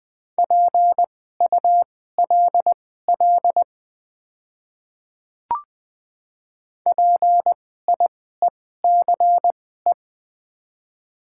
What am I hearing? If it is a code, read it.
Morse code, heard as a keyed tone, about 20 wpm, spaced as for 14 wpm, 690 Hz. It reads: PULL PIECE
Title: 2x Character Spacing - Top 500 Words - 20wpm